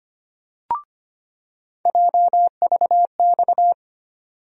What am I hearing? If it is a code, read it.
Morse code, heard as a keyed tone, about 25 wpm, 700 Hz. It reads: JVX
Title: USA Call Sign Postfixes - 3 Letters 25wpm